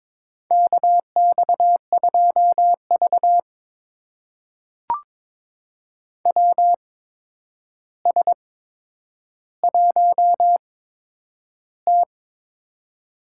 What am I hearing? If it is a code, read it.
Morse code, heard as a keyed tone, about 22 wpm, spaced as for 9 wpm, 700 Hz. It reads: KX2V WS1T
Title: USA Call Signs - 4 Characters - 8x Character Spacing - 22wpm